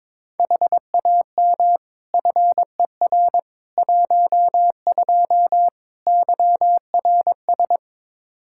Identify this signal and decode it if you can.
Morse code, heard as a keyed tone, about 21 wpm, 700 Hz. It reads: HAM FER 12 YRS
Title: Practice QSO - Exchange 3 - More Info - 22wpm